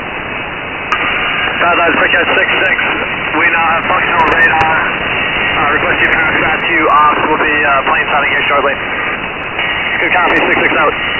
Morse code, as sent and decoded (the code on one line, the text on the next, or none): none